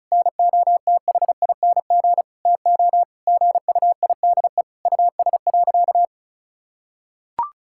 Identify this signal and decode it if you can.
Morse code, heard as a keyed tone, about 36 wpm, 700 Hz. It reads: NOTHING TO GUIDE US.
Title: Sentences from Top 1000 Words - Review Part 5 - 35wpm